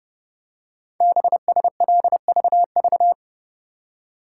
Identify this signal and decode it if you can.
Morse code, heard as a keyed tone, about 28 wpm, 700 Hz. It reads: BSLVV
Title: Code Groups - 5 Letters - 30wpm